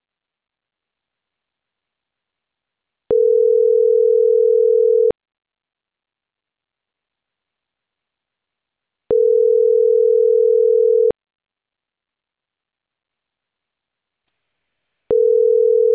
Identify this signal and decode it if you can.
Ringback tone (North America), 440 + 480 Hz.